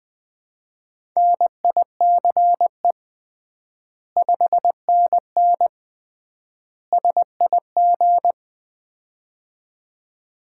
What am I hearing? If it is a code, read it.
Morse code, heard as a keyed tone, about 20 wpm, 700 Hz. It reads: NICE 5NN SIG
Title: Common QSO Phrases - 3x Word-Spacing - 20wpm